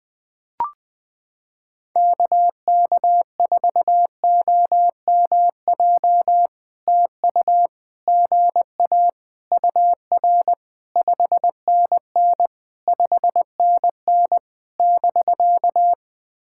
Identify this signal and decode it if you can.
Morse code, heard as a keyed tone, about 20 wpm, 660 Hz. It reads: KK4OMJ TU GA UR 5NN 5NN <BK>
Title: POTA QSO Practice - 20wpm